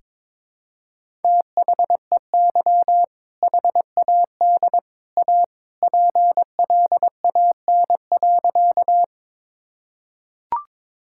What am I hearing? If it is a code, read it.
Morse code, heard as a keyed tone, about 22 wpm, 700 Hz. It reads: THEY HAD A PLAN.
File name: practice / 22wpm / Sentences with 2-Word Phrases - 22wpm